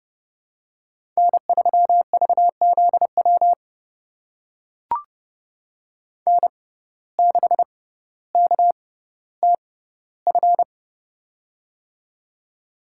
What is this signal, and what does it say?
Morse code, heard as a keyed tone, about 30 wpm, spaced as for 15 wpm, 700 Hz. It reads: N3VZW N6KTF
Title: USA Call Signs - 5 Characters - 6x Character Spacing - 30wpm